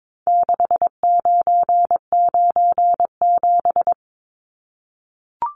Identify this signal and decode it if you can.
Morse code, heard as a keyed tone, about 22 wpm, 700 Hz. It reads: 6997
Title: Numbers - 4-Digit 22wpm